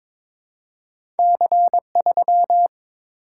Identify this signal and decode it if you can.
Morse code, heard as a keyed tone, about 22 wpm, 700 Hz. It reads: C3
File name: practice / 22wpm / Two Letter-Number Combinations - 22wpm